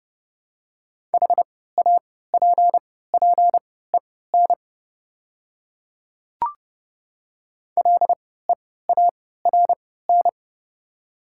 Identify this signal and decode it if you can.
Morse code, heard as a keyed tone, about 30 wpm, spaced as for 17 wpm, 700 Hz. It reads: HAPPEN LEARN
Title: 3x Character Spacing - Top 500 Words - 30wpm